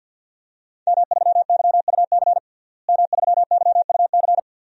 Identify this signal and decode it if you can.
Morse code, heard as a keyed tone, about 52 wpm, 700 Hz. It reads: M3YUC M3YUC
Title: DXCC-100 Call Signs - 5 Characters - Repeated 3x - 50wpm